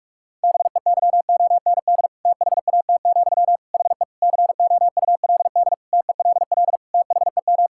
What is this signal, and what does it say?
Morse code, heard as a keyed tone, about 47 wpm, 700 Hz. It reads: BEYOND THAT, HE COULD TELL THEM
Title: Sentences from Top 1000 Words - Review Part 6 - 45wpm